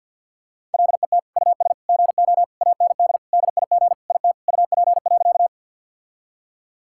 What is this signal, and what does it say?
Morse code, heard as a keyed tone, about 50 wpm, 700 Hz. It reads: LET US GO AND DIG IT UP.
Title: Sentences from Top 1600 Words 50wpm